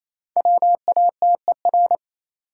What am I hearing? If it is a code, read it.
Morse code, heard as a keyed tone, about 28 wpm, 700 Hz. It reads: WATER
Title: Five Letter Words - 28wpm